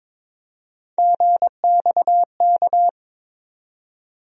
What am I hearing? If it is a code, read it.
Morse code, heard as a keyed tone, about 22 wpm, 700 Hz. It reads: GXK
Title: USA Call Sign Postfixes - 3 Letters 22wpm